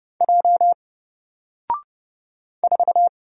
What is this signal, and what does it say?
Morse code, heard as a keyed tone, about 30 wpm, 700 Hz. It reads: J 4
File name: Single-Letter-Number-30wpm